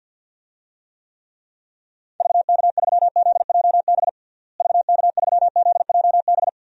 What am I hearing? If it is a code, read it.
Morse code, heard as a keyed tone, about 50 wpm, 700 Hz. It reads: VK3ZJB VK3ZJB
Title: DXCC-100 Call Signs - 6 Characters - Repeated 2x - 50wpm